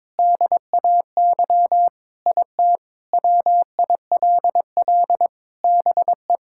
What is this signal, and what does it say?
Morse code, heard as a keyed tone, about 21 wpm, 700 Hz. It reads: DAY IT WILL BE
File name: practice / 22wpm / Sentences with 4-Word Phrases - 22wpm